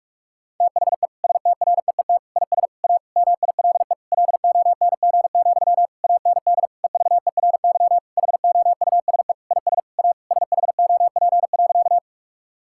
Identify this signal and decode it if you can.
Morse code, heard as a keyed tone, about 42 wpm, 700 Hz. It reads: THE STREET IS A MILE LONG, AND EVERY HOUSE IS A SHOP.
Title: Sentences from Top 1600 Words 45wpm